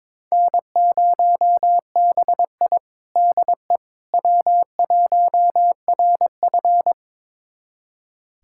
Morse code, heard as a keyed tone, about 22 wpm, 700 Hz. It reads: N0BI DE W1RF
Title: USA Callsign DE Callsign - 4 Characters - 22wpm